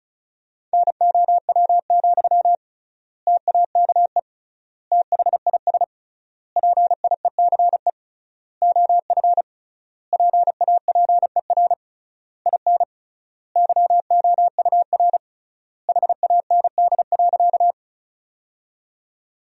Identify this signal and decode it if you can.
Morse code, heard as a keyed tone, about 36 wpm, 700 Hz. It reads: NOW, TAKE THIS PIECE OF PAPER IN YOUR HAND.
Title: 3x Word-Spacing - Sentences from Top 400 Words - 35wpm